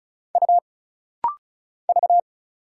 Morse code, heard as a keyed tone, about 35 wpm, 700 Hz. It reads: U V
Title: Single Letters - Rapid-Fire 35wpm